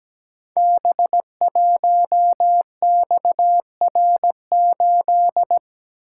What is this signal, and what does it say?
Morse code, heard as a keyed tone, about 17 wpm, 700 Hz. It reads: B1XR8
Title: Code Groups - 5 Letter:Numbers - 17wpm